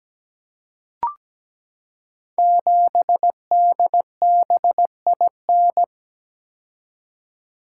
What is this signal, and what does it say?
Morse code, heard as a keyed tone, about 17 wpm, 700 Hz. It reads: 7DBIN
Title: Code Groups - 5 Letter:Numbers - 17wpm